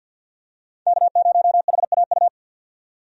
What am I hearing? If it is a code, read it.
Morse code, heard as a keyed tone, about 52 wpm, 700 Hz. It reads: K0HAU
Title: USA Call Signs - 5 Characters - 50wpm